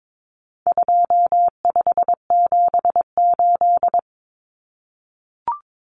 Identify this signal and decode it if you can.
Morse code, heard as a keyed tone, about 22 wpm, 700 Hz. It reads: 2578
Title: Common Cut Numbers - 4-Digit 22wpm